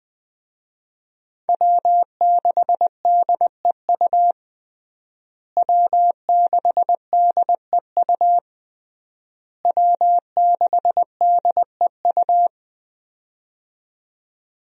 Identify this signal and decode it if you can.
Morse code, heard as a keyed tone, about 21 wpm, 700 Hz. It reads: W6DEU W6DEU W6DEU
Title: USA Call Signs - 5 Characters - Repeated 3x - 20wpm